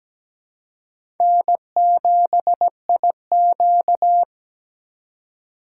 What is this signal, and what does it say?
Morse code, heard as a keyed tone, about 17 wpm, 700 Hz. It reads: N7IQ
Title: USA Call Signs - 2x Character Spacing - 17wpm